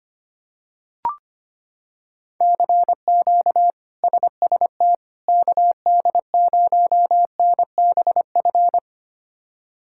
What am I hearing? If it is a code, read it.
Morse code, heard as a keyed tone, about 25 wpm, 700 Hz. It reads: CQ SST KD0NBF
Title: K1USN SST QSO Practice - 25wpm